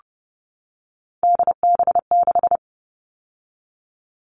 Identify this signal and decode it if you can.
Morse code, heard as a keyed tone, about 30 wpm, 700 Hz. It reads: DB6
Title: All Cut Numbers - 3-Digit 30wpm